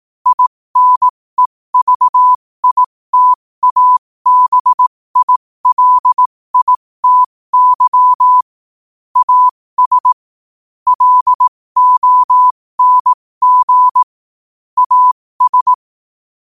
Morse code, heard as a keyed tone, about 18 wpm, spaced as for 15 wpm, 1000 Hz. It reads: INEVITABILITY AS LONG AS
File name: qotdfiles / 15wpm